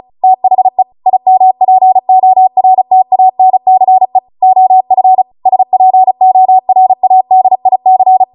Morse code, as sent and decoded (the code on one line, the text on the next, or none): - .... / .. -- .--. --- .-. - .- -. -.-. / --- ..-. / ... .--. --- .-. .- -.. .. -.-.
THE IMPORTANCE OF SPORADIC